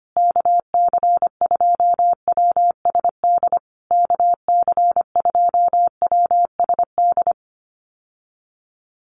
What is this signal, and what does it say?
Morse code, heard as a keyed tone, about 25 wpm, 700 Hz. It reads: KC2WSD KC2WSD
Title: USA Call Signs Repeated 3x 25wpm